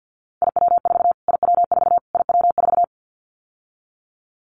Morse code, heard as a keyed tone, about 50 wpm, 700 Hz. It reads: IW4 IW4 IW4
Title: Callsign Prefixes for Top 25 DXCC - Repeated 3x - 50wpm